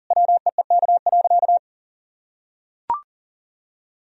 Morse code, heard as a keyed tone, about 40 wpm, 700 Hz. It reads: WEEK.
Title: Sentences from Top 1000 Words - Review Part 2 - 40wpm